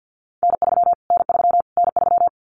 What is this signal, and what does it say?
Morse code, heard as a keyed tone, about 50 wpm, 700 Hz. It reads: N3 N3 N3
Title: Callsign Prefixes for Top 25 DXCC - Repeated 3x - 50wpm